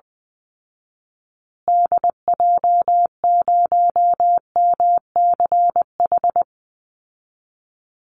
Morse code, heard as a keyed tone, about 20 wpm, 700 Hz. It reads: DJ0MCH
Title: Call Signs Repeated 2x - 20wpm